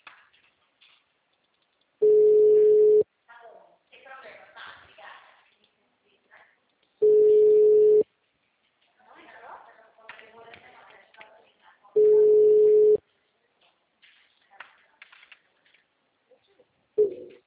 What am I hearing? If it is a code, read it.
Ringback tone (Europe, 425 Hz), with speech.
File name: out- -2008- - - .116